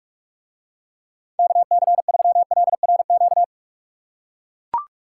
Morse code, heard as a keyed tone, about 44 wpm, 700 Hz. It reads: KC3LRQ